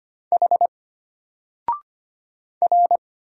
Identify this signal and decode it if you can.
Morse code, heard as a keyed tone, about 25 wpm, 700 Hz. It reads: H R
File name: Single-Letter-Number-25wpm